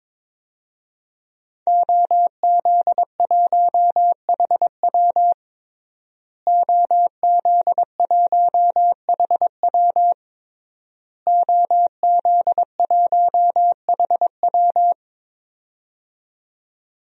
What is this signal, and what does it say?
Morse code, heard as a keyed tone, about 22 wpm, 700 Hz. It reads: OZ1HW OZ1HW OZ1HW
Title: DXCC-100 Call Signs - 5 Characters - Repeated 3x - 22wpm